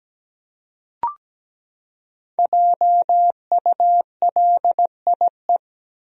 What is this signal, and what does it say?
Morse code, heard as a keyed tone, about 17 wpm, 700 Hz. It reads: JULIE
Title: USA Common Names - 17wpm